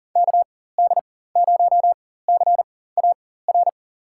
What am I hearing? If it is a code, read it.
Morse code, heard as a keyed tone, about 40 wpm, spaced as for 19 wpm, 700 Hz. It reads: KD0CAR
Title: USA Call Signs - 4x Character Spacing - 40wpm